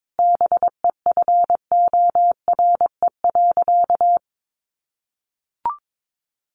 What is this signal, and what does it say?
Morse code, heard as a keyed tone, about 22 wpm, 700 Hz. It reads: BEFORE.
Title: Sentences from Top 1100 Words 22wpm